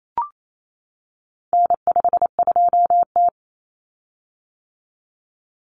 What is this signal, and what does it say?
Morse code, heard as a keyed tone, about 28 wpm, 700 Hz. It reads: N52T
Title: Common Cut Numbers - 4-Digit 28wpm